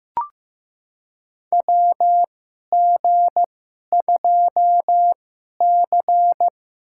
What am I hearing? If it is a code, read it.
Morse code, heard as a keyed tone, about 15 wpm, spaced as for 11 wpm, 700 Hz. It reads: WG2C
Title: USA Call Signs - 2x Character Spacing - 15wpm